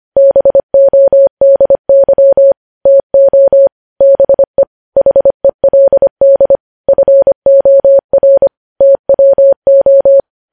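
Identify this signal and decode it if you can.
Morse code, heard as a keyed tone, about 25 wpm, 550 Hz. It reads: BODY TO BE HELD FOR TWO
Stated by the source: Morse code news headlines at 25 wpm.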